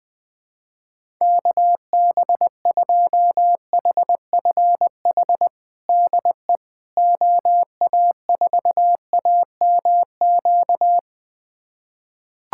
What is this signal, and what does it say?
Morse code, heard as a keyed tone, about 20 wpm, 700 Hz. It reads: KB2HFH DE OA4AMQ
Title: USA Callsign DE Callsign - 6 Characters - 20wpm